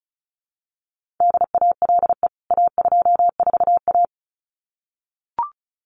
Morse code, heard as a keyed tone, about 34 wpm, 700 Hz. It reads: DALE A24A
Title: CWT Exchange 35wpm